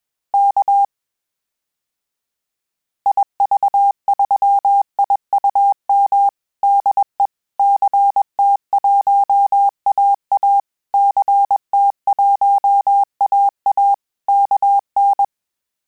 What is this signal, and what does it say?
Morse code, heard as a keyed tone, about 21 wpm, 800 Hz. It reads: K IV3IUM DE CT1AA CT1AA KN
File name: QSO CW 100 CPM